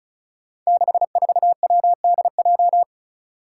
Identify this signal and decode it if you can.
Morse code, heard as a keyed tone, about 34 wpm, 700 Hz. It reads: 64WDJ
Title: Five Challenging Letter-Number Combinations 35wpm